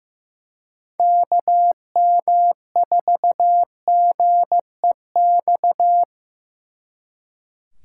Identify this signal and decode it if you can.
Morse code, heard as a keyed tone, about 18 wpm, 700 Hz. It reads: KM4GEX
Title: Speed Racing - USA 2x3 Call Signs 15wpm